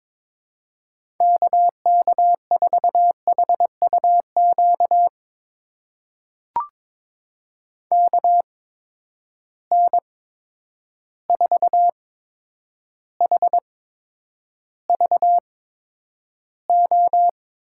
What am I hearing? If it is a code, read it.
Morse code, heard as a keyed tone, about 22 wpm, spaced as for 9 wpm, 700 Hz. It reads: KK4HUQ KN4HVO
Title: USA Call Signs - 6 Characters - 8x Character Spacing - 22wpm